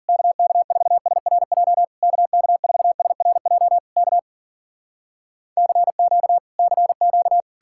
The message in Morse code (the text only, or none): KK4SRJ KK4SRJ K CQ CQ